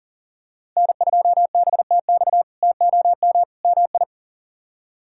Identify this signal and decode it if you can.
Morse code, heard as a keyed tone, about 40 wpm, 700 Hz. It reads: N2BTX TOM MI
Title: K1USN SST QSO Practice - 40wpm